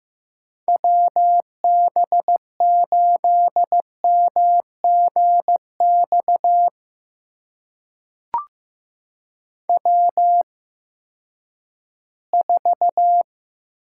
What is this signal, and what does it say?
Morse code, heard as a keyed tone, about 15 wpm, spaced as for 10 wpm, 700 Hz. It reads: WB8MGX W4
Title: USA Call Signs - 8x Character Spacing - 15wpm